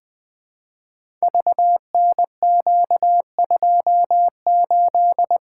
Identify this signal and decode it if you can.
Morse code, heard as a keyed tone, about 21 wpm, 700 Hz. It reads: VNQ28
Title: Code Groups - 5 Letter:Numbers - 20wpm